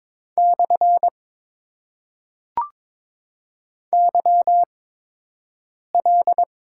Morse code, heard as a keyed tone, about 22 wpm, spaced as for 6 wpm, 700 Hz. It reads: / YL
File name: Top 90 QSO Elements - 8x Character Spacing - 22wpm